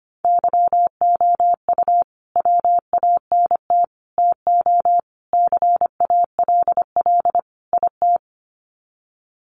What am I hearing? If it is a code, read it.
Morse code, heard as a keyed tone, about 25 wpm, 700 Hz. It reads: YOU WANT TO CALL IT
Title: Sets of 5 Words Encore2 25wpm